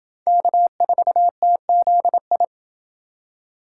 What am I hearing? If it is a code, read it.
Morse code, heard as a keyed tone, about 28 wpm, 700 Hz. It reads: K4TZI